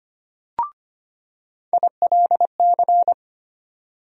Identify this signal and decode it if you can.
Morse code, heard as a keyed tone, about 25 wpm, 700 Hz. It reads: ILC
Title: USA Call Sign Postfixes - 3 Letters 25wpm